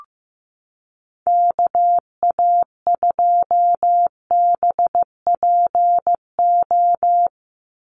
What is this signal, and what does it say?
Morse code, heard as a keyed tone, about 15 wpm, 700 Hz. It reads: KA2BPO
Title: USA Call Signs 15wpm